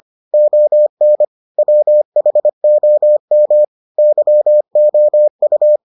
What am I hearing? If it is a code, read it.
Morse code, heard as a keyed tone, about 25 wpm, 600 Hz. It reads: ON WHOM YOU